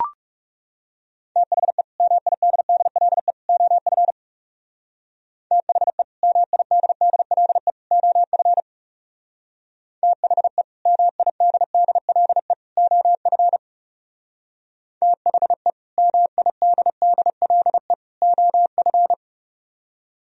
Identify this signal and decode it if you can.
Morse code, heard as a keyed tone, about 37 wpm, 700 Hz. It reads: THE MIDDLE OF THE MIDDLE OF THE MIDDLE OF THE MIDDLE OF
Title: Speed-Racing - Sets of 3 Words 30wpm